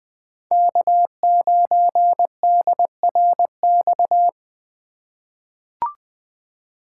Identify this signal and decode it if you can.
Morse code, heard as a keyed tone, about 21 wpm, 700 Hz. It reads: K9DRX
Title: USA Call Signs - 5 Characters - 2x Character Spacing - 20wpm